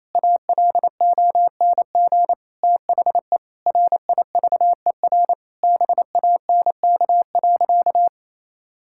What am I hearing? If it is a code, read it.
Morse code, heard as a keyed tone, about 28 wpm, 700 Hz. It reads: ALONG THE RIVER BANK.
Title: Sentences from Top 1000 Words - Review Part 6 - 28wpm